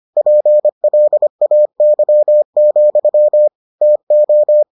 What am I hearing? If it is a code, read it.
Morse code, heard as a keyed tone, about 25 wpm, 600 Hz. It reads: PLAY, TO